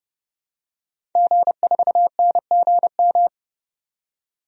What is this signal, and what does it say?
Morse code, heard as a keyed tone, about 30 wpm, 700 Hz. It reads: G4NGM
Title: DXCC-100 Call Signs - 5 Characters - 30wpm